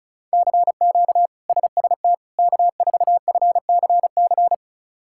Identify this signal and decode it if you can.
Morse code, heard as a keyed tone, about 36 wpm, 700 Hz. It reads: CQ SST K4FCC
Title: K1USN SST QSO Practice - 35wpm